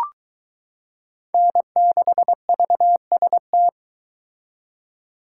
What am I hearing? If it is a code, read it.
Morse code, heard as a keyed tone, about 22 wpm, 700 Hz. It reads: N6VST